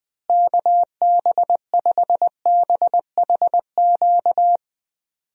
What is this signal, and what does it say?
Morse code, heard as a keyed tone, about 21 wpm, 700 Hz. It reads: KB5BHQ
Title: USA Call Signs - 6 Characters - Repeated 3x - 20wpm